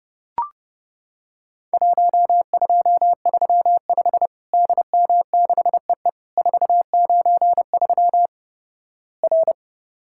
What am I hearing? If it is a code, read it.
Morse code, heard as a keyed tone, about 29 wpm, 700 Hz. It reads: 1235 DM6EE 493 R
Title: Worked All Europe DX Contest - QTC Exchange - 30wpm